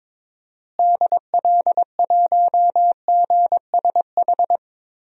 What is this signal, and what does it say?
Morse code, heard as a keyed tone, about 22 wpm, 700 Hz. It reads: DL1GSH
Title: DXCC-100 Call Signs - 6 Characters - Repeated 2x - 22wpm